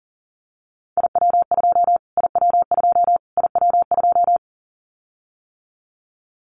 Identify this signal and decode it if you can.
Morse code, heard as a keyed tone, about 40 wpm, 700 Hz. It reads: IW2 IW2 IW2
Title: Callsign Prefixes for Top 25 DXCC - Repeated 3x - 40wpm